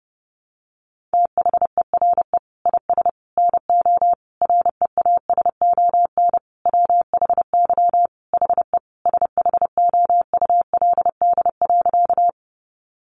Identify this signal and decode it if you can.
Morse code, heard as a keyed tone, about 30 wpm, 700 Hz. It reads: THERE IS NO REASON WHY HE SHOULD.
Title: Sentences from Top 700 Words 30wpm